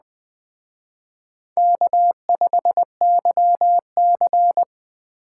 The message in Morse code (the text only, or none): K5YC